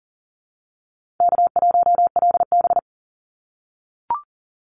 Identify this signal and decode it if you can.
Morse code, heard as a keyed tone, about 40 wpm, 700 Hz. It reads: K1LB
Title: Easy Call Signs - 40wpm